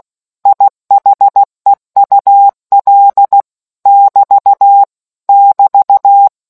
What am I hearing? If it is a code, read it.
Morse code, heard as a keyed tone, about 16 wpm, 800 Hz. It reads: IHEUL <BT> <BT>